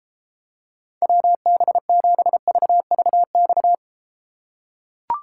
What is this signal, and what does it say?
Morse code, heard as a keyed tone, about 32 wpm, 700 Hz. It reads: WB7VVX